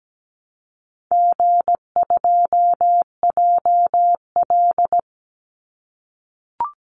Morse code, heard as a keyed tone, about 17 wpm, 700 Hz. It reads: G2JL
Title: Call Signs - 17wpm